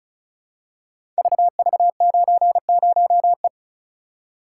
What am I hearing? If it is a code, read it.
Morse code, heard as a keyed tone, about 36 wpm, 690 Hz. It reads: VV90E
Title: Code Groups - 5 Letter:Numbers - 35wpm